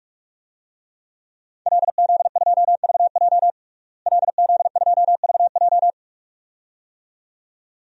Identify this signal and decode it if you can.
Morse code, heard as a keyed tone, about 46 wpm, 700 Hz. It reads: LZ2VJ LZ2VJ
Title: DXCC-100 Call Signs - 5 Characters - Repeated 2x - 45wpm